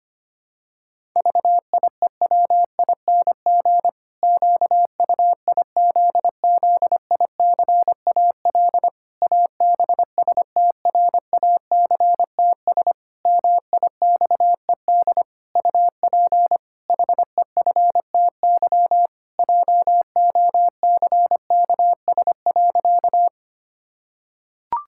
Morse code, heard as a keyed tone, about 25 wpm, 700 Hz. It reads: VIEWING QUIZZICAL ABSTRACTS MIXED UP HEFTY JOCKS.
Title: Pangram Challenge v2 - 25wpm